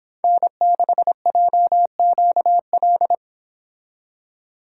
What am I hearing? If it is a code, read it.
Morse code, heard as a keyed tone, about 26 wpm, 700 Hz. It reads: N6JQL
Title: Speed Racing - USA 1x3 Call Signs 22wpm